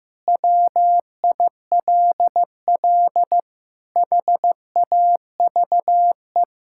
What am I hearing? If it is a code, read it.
Morse code, heard as a keyed tone, about 15 wpm, 700 Hz. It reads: WILL HAVE
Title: Sentences from Top 1000 Words - Review Part 6 - 15wpm